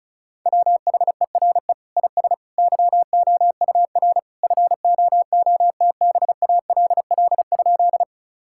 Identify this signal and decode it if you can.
Morse code, heard as a keyed tone, about 34 wpm, 700 Hz. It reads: WHERE IS YOUR FOOTBALL?
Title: Sentences from Top 1600 Words 35wpm